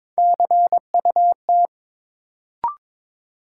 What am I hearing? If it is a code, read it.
Morse code, heard as a keyed tone, about 22 wpm, 700 Hz. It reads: CUT
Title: Sentences from Top 300 Words - New Word Drill - 22wpm